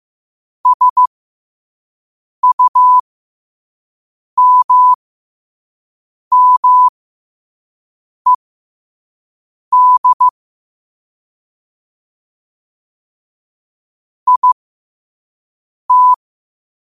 Morse code, heard as a keyed tone, about 15 wpm, spaced as for 5 wpm, 1000 Hz. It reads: SUMMED IT